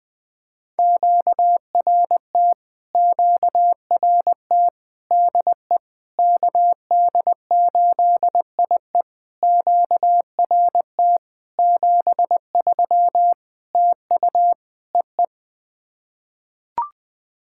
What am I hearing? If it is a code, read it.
Morse code, heard as a keyed tone, about 20 wpm, 700 Hz. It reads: QRT QRT DE KD8IE QRT 73 TU EE
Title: POTA QSO Practice - 20wpm